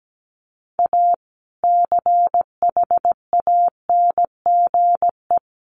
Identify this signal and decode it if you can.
Morse code, heard as a keyed tone, about 17 wpm, 700 Hz. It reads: A CHANGE
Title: Sentences from Top 1400 Words 17wpm